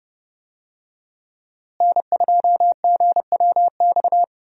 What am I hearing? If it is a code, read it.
Morse code, heard as a keyed tone, about 30 wpm, 700 Hz. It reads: N2GWX
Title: USA Call Signs - 5 Characters - Repeated 2x - 30wpm